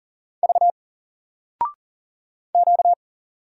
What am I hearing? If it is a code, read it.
Morse code, heard as a keyed tone, about 40 wpm, 700 Hz. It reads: V Q
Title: Single Letter-Number 40wpm